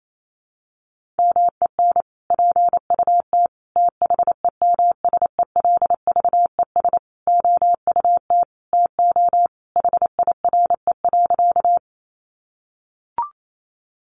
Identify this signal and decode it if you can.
Morse code, heard as a keyed tone, about 28 wpm, 700 Hz. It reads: MEN PUT THEMSELVES OUT TO HIRE.
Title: Sentences from Top 1400 Words 28wpm